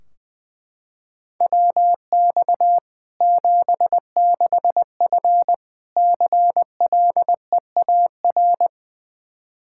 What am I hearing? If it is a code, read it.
Morse code, heard as a keyed tone, about 21 wpm, 700 Hz. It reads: WX 76F CLEAR
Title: K1USN SST QSO Practice 20wpm